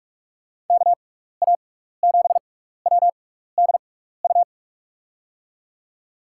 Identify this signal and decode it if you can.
Morse code, heard as a keyed tone, about 44 wpm, spaced as for 15 wpm, 700 Hz. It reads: KA7WDU
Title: USA Call Signs - 6 Characters - 6x Character Spacing - 45wpm